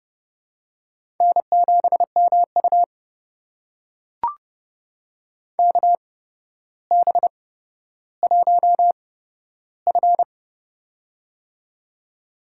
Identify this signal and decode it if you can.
Morse code, heard as a keyed tone, about 30 wpm, spaced as for 13 wpm, 700 Hz. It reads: N7MU KB1F
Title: USA Call Signs - 8x Character Spacing - 30wpm